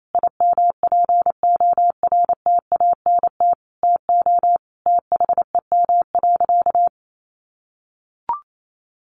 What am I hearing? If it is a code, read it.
Morse code, heard as a keyed tone, about 28 wpm, 700 Hz. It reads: IMPORTANT TO THEM.
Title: Sentences from Top 1400 Words 28wpm